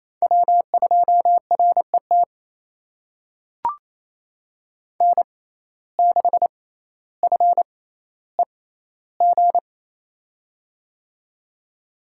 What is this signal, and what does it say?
Morse code, heard as a keyed tone, about 28 wpm, spaced as for 14 wpm, 700 Hz. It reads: W2RET N6FEG
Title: USA Call Signs - 5 Characters - 6x Character Spacing - 28wpm